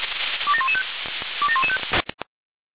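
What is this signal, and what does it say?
Five-tone selective call (ZVEI), heard as quick runs of five tones, tones of about 70 ms. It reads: 27115 27115
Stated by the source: ZVEI five-tone selective call 27084, received on SDR.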